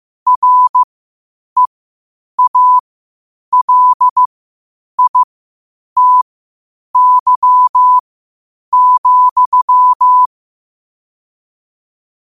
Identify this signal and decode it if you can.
Morse code, heard as a keyed tone, about 15 wpm, spaced as for 8 wpm, 1000 Hz. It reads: REALITY,
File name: qotdfiles / 8wpm